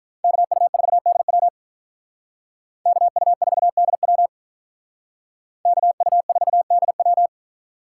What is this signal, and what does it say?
Morse code, heard as a keyed tone, about 45 wpm, 700 Hz. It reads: KU4DW KU4DW KU4DW